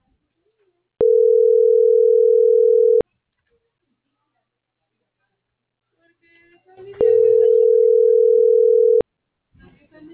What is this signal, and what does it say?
Ringback tone (North America), 440 + 480 Hz.